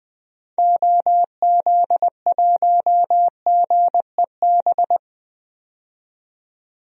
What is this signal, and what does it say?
Morse code, heard as a keyed tone, about 20 wpm, 700 Hz. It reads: OZ1GEB
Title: DXCC-100 Call Signs - 6 Characters - 20wpm